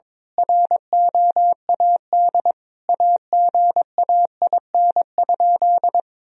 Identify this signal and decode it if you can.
Morse code, heard as a keyed tone, about 22 wpm, 700 Hz. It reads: ROAD AGAIN?
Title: Sentences from Top 1000 Words - Review Part 2 - 22wpm